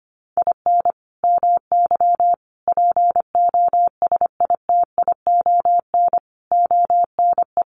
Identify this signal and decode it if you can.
Morse code, heard as a keyed tone, about 25 wpm, 700 Hz. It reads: IN MY POSITION ONE
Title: Sentences from Top 1100 Words 25wpm